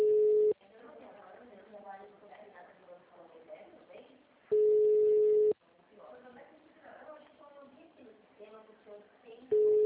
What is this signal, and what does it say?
Ringback tone (Europe, 425 Hz), with speech.
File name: out- -2004- - - .1396